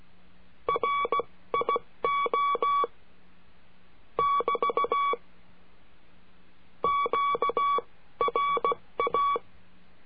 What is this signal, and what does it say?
Morse code, heard as a keyed tone, about 16 wpm, spaced as for 14 wpm, 1130 Hz. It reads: RIO <BT> QRA
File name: cursos / CURSO DE TELEGRAFIA / Telegrafia - Trilha 22